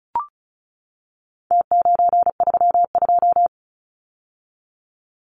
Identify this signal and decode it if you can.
Morse code, heard as a keyed tone, about 34 wpm, 700 Hz. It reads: T932
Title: Numbers - Contest Style - 4-Digit 35wpm